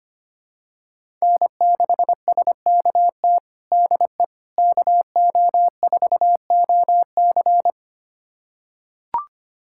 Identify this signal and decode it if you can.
Morse code, heard as a keyed tone, about 25 wpm, 700 Hz. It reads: N6SKT DE KO4OC
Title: USA Callsign DE Callsign - 5 Characters - 25wpm